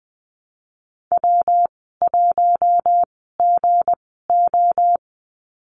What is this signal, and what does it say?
Morse code, heard as a keyed tone, about 20 wpm, spaced as for 15 wpm, 700 Hz. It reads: W1GO
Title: 167 Lesson 27g - Number 7 - Call Signs 20wpm